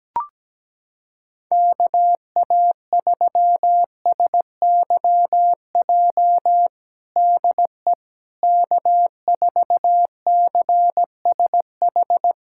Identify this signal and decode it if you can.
Morse code, heard as a keyed tone, about 17 wpm, 700 Hz. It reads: KA3SYJ DE K4CSH
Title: USA Callsign DE Callsign 17wpm